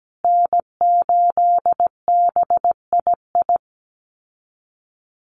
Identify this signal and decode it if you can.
Morse code, heard as a keyed tone, about 17 wpm, 700 Hz. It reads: N8BII
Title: USA Call Signs Repeated 2x 17wpm